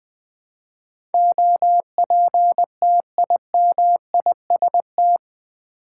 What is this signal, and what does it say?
Morse code, heard as a keyed tone, about 20 wpm, 700 Hz. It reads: OPTIMIST
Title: Build A Word - 4 Letters At A Time - 10-Letter Words - 20wpm